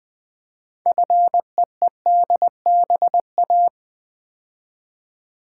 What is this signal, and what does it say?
Morse code, heard as a keyed tone, about 20 wpm, 700 Hz. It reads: FEEDBA
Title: Build A Word - 3 Letters At A Time - 8-Letter Words - 20wpm